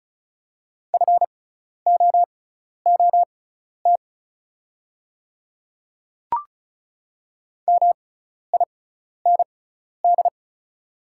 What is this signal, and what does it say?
Morse code, heard as a keyed tone, about 34 wpm, spaced as for 12 wpm, 700 Hz. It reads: FOOT MIND
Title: 6x Character Spacing - Top 500 Words - 35wpm